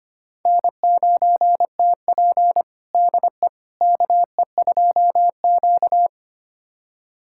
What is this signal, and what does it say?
Morse code, heard as a keyed tone, about 25 wpm, 700 Hz. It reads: N9TP DE KE2Q
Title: USA Callsign DE Callsign - 4 Characters - 25wpm